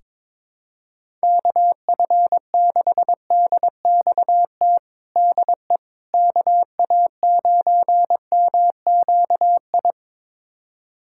Morse code, heard as a keyed tone, about 22 wpm, 700 Hz. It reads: KF6DXT DE KA9MQI
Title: USA Callsign DE Callsign - 6 Characters - 22wpm